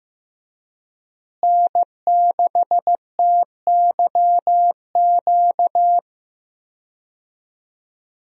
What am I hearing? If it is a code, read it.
Morse code, heard as a keyed tone, about 15 wpm, 700 Hz. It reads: N6TYQ
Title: USA Call Signs - 5 Characters - Repeated 2x - 15wpm